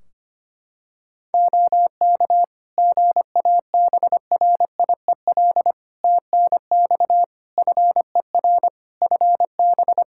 Morse code, heard as a keyed tone, about 25 wpm, 700 Hz. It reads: OK GABRIEL TNX FER FB
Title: Practice QSO - Exchange 4 - Ending - 25wpm